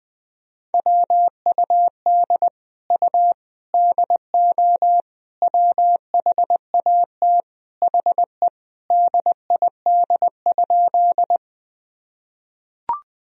Morse code, heard as a keyed tone, about 21 wpm, 700 Hz. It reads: WUD U DO WHAT HE DID?
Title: Sentences from Top 100 Words and Abbreviations 20wpm